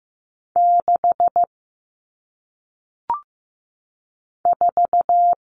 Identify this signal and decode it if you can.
Morse code, heard as a keyed tone, about 15 wpm, 700 Hz. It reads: 6 4
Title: Numbers - 1-Digit 15wpm